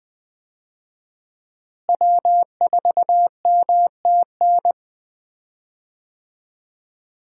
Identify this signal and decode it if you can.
Morse code, heard as a keyed tone, about 20 wpm, 700 Hz. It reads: W4MTN
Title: USA Call Signs - 5 Characters - Repeated 2x - 20wpm